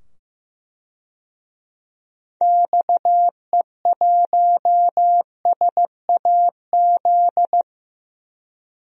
Morse code, heard as a keyed tone, about 15 wpm, 700 Hz. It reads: XE1SAZ
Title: DXCC-100 Call Signs - 6 Characters - 15wpm